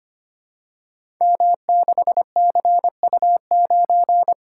Morse code, heard as a keyed tone, about 25 wpm, 700 Hz. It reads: M6CU9
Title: Code Groups - 5 Letter:Numbers - 25wpm